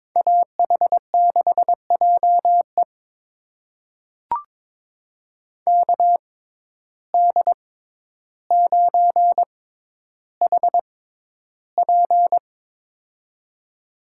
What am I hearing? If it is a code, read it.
Morse code, heard as a keyed tone, about 22 wpm, spaced as for 11 wpm, 700 Hz. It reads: AH6JE KD9HP
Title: USA Call Signs - 5 Characters - 6x Character Spacing - 22wpm